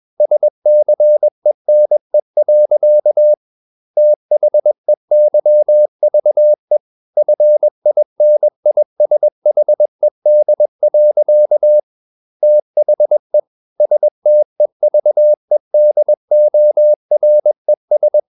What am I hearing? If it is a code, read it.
Morse code, heard as a keyed tone, about 21 wpm, 600 Hz. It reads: SCENE. THEYVE FINISHED. THE STEVEDORES
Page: CW med Gnister